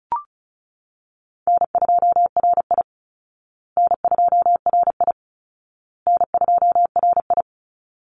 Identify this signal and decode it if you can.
Morse code, heard as a keyed tone, about 34 wpm, 700 Hz. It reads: N2RI N2RI N2RI
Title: Easy Call Signs Repeated 3x - 35wpm